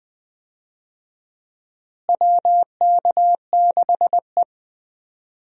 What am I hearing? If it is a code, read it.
Morse code, heard as a keyed tone, about 20 wpm, 700 Hz. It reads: WK6E
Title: USA Call Signs - 4 Characters - Repeated 3x - 20wpm